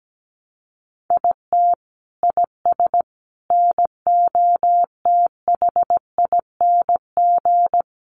Morse code, heard as a keyed tone, about 17 wpm, 700 Hz. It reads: IT IS NOTHING
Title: Sentences from Top 1100 Words 17wpm